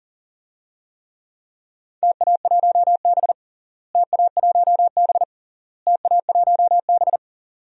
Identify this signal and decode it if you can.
Morse code, heard as a keyed tone, about 40 wpm, 700 Hz. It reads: TA1B TA1B TA1B
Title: DXCC-100 Call Signs - 4 Characters - Repeated 3x - 40wpm